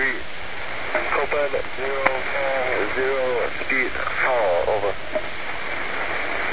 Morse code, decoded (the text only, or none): none